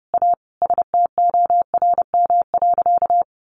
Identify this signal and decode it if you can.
Morse code, heard as a keyed tone, about 28 wpm, 700 Hz. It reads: A STORM.
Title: Sentences from Top 1500 Words 30wpm